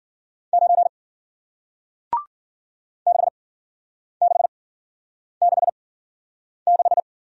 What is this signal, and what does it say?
Morse code, heard as a keyed tone, about 54 wpm, spaced as for 9 wpm, 700 Hz. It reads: 9 6666